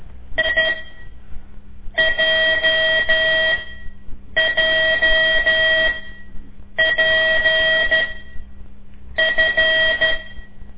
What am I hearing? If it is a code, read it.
Morse code, heard as a keyed tone, about 13 wpm, spaced as for 6 wpm, 640 Hz, with noise. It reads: IJJPF
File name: ik1weg2 / CORSI CW / lez5-2